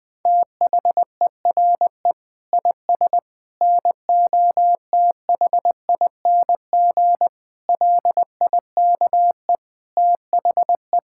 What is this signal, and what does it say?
Morse code, heard as a keyed tone, about 20 wpm, 700 Hz. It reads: THERE IS NOTHING LIKE THE
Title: Difficult Sentences with Hints - 20wpm